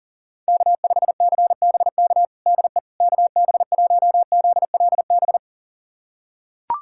Morse code, heard as a keyed tone, about 40 wpm, 700 Hz. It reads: K5CBK DE KB1ZLB
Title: USA Callsign DE Callsign 40wpm